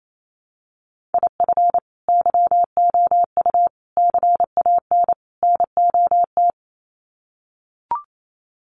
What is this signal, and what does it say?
Morse code, heard as a keyed tone, about 28 wpm, 700 Hz. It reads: IF YOU CAN NOT
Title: Sets of 4 Words 28wpm